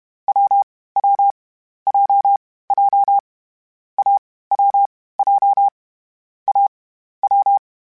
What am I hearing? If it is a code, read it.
Morse code, heard as a keyed tone, about 31 wpm, 800 Hz. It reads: W W J J A W J A W